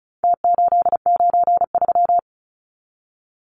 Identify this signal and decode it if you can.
Morse code, heard as a keyed tone, about 34 wpm, 700 Hz. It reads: T893
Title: Numbers - Contest Style - 4-Digit 35wpm